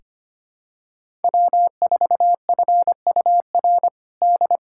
Morse code, heard as a keyed tone, about 25 wpm, 700 Hz. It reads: W4FUR D
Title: USA Callsign DE Callsign - 5 Characters - 25wpm